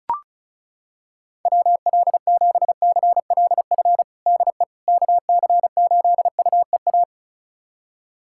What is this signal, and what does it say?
Morse code, heard as a keyed tone, about 36 wpm, 690 Hz. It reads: WL7CLF DE KC8UEA
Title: USA Callsign DE Callsign - 6 Characters - 35wpm